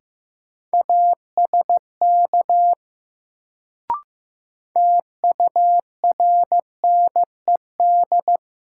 Morse code, heard as a keyed tone, about 15 wpm, 700 Hz. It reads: ASK TURNED
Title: Sets of 4 Words - New Word Drill - 15wpm